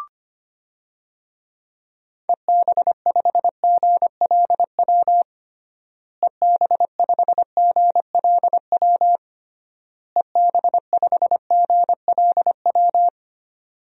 Morse code, heard as a keyed tone, about 25 wpm, 700 Hz. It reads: EB5GLW EB5GLW EB5GLW
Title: DXCC-100 Call Signs - 6 Characters - Repeated 3x - 25wpm